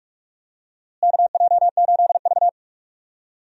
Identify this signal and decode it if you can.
Morse code, heard as a keyed tone, about 45 wpm, 700 Hz. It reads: KJ8V
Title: USA Call Signs - 4 Characters - Repeated 3x - 45wpm